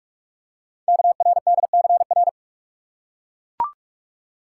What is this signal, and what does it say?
Morse code, heard as a keyed tone, about 44 wpm, 700 Hz. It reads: KRDCR
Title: Code Groups - 5 Letters - 45wpm